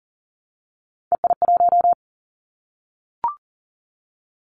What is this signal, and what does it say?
Morse code, heard as a keyed tone, about 40 wpm, 700 Hz. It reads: EI1
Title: Callsign Prefixes for Top 50 DXCC - 40wpm